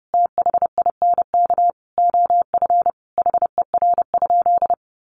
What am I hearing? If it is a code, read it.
Morse code, heard as a keyed tone, about 28 wpm, 700 Hz. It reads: THINK OF HER?
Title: Sentences from Top 200 Words 30wpm